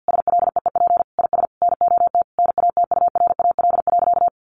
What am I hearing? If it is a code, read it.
Morse code, heard as a keyed tone, about 50 wpm, 700 Hz. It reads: SLEEP IS NOT NATURAL.
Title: Sentences from Top 700 Words 50wpm